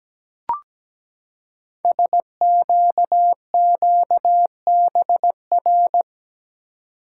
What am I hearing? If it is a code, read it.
Morse code, heard as a keyed tone, about 17 wpm, 700 Hz. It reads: SQQBR
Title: Code Groups - 5 Letters - 17wpm